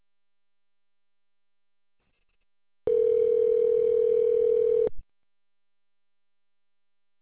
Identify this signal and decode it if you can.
Ringback tone (North America), 440 + 480 Hz.